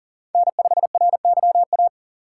Morse code, heard as a keyed tone, about 40 wpm, 700 Hz. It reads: N5RYA
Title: USA Call Signs - 5 Characters - 3x Character Spacing - 40wpm